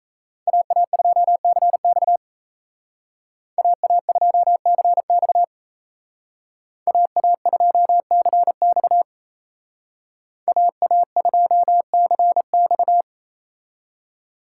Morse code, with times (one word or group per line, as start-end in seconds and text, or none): 0.5-2.2 AA2CX
3.6-5.4 AA2CX
6.9-9.0 AA2CX
10.5-13.0 AA2CX